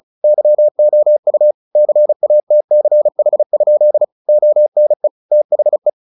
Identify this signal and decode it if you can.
Morse code, heard as a keyed tone, about 34 wpm, 600 Hz. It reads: YOU CATCH? ONE THE
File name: the old man and the sea 2 35wpm